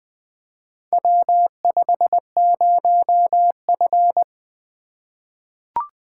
Morse code, heard as a keyed tone, about 21 wpm, 700 Hz. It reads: W50F
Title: Four Letter-Number Combinations - 20wpm